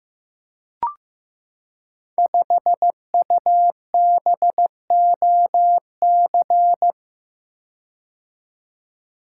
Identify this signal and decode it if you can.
Morse code, heard as a keyed tone, about 15 wpm, 700 Hz. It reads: 5UBOC
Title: Code Groups - 5 Letter:Numbers - 15wpm